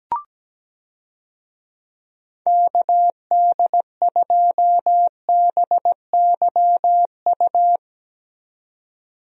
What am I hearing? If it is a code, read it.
Morse code, heard as a keyed tone, about 17 wpm, 700 Hz. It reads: KD2BYU
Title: USA Call Signs - 6 Characters - 17wpm